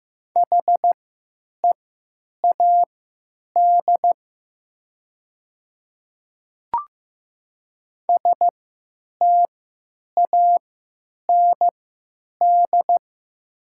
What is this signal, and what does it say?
Morse code, heard as a keyed tone, about 15 wpm, spaced as for 9 wpm, 700 Hz. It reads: HEAD STAND
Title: 3x Character Spacing - Top 500 Words - 15wpm